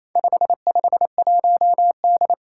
Morse code, heard as a keyed tone, about 28 wpm, 700 Hz. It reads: 551D
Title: Four Challenging Letter-Number Combinations 28wpm